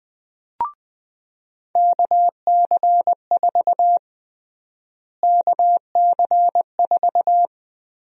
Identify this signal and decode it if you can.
Morse code, heard as a keyed tone, about 20 wpm, 700 Hz. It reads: KC4 KC4
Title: USA Call Sign Prefixes - Repeated 2x - 20wpm